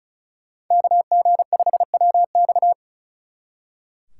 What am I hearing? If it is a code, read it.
Morse code, heard as a keyed tone, about 45 wpm, 700 Hz. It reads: KG5WX